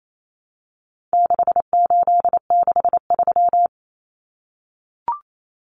Morse code, heard as a keyed tone, about 28 wpm, 700 Hz. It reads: 6863
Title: Common Cut Numbers - 4-Digit 28wpm